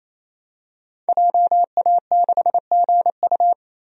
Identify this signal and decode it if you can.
Morse code, heard as a keyed tone, about 28 wpm, 700 Hz. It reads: JA6GU
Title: DXCC-100 Call Signs - 5 Characters - 28wpm